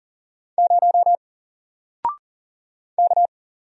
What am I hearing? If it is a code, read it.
Morse code, heard as a keyed tone, about 40 wpm, 700 Hz. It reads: 0 K
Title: Single Character (Letter:Number:Punctuation:Prosign) 40wpm